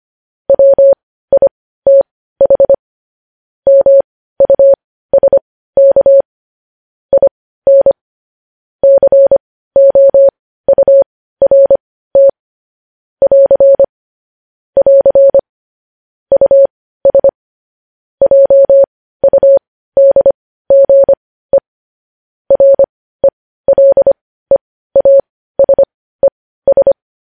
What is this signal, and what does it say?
Morse code, heard as a keyed tone, about 25 wpm, spaced as for 15 wpm, 550 Hz. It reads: WITH MUSK IN COURT <AR> <AR> US JUDGE RELEASES
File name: podcast / News headlines-1x-25x15